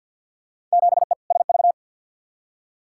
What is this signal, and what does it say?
Morse code, heard as a keyed tone, about 52 wpm, 700 Hz. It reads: 7E SV
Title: Field Day Exchanges Encore - 50wpm